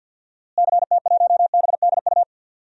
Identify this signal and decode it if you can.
Morse code, heard as a keyed tone, about 51 wpm, 700 Hz. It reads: CT1BDU
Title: DXCC-100 Call Signs - 6 Characters - Repeated 2x - 50wpm